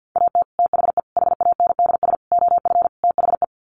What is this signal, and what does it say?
Morse code, heard as a keyed tone, about 50 wpm, 700 Hz. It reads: AT THE HANDS OF THE
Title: Sets of 5 Words Encore2 50wpm